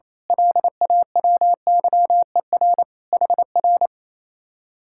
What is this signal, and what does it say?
Morse code, heard as a keyed tone, about 26 wpm, 700 Hz. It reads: LAWYER HR
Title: Speed-Racing Common QSO Phrases - 28wpm